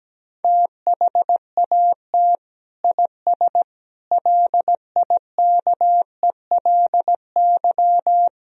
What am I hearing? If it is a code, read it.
Morse code, heard as a keyed tone, about 17 wpm, 700 Hz. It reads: THAT IS LIKELY
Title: Sentences from Top 1000 Words - Review Part 4 - 17wpm